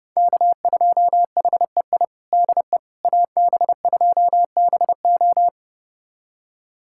Morse code, heard as a keyed tone, about 32 wpm, 700 Hz. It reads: K2HEI DE AB2BO
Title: USA Callsign DE Callsign - 5 Characters - 30wpm